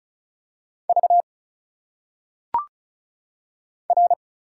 Morse code, heard as a keyed tone, about 36 wpm, 700 Hz. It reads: V R
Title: Single Letters - Phonetic Pronunciation 35wpm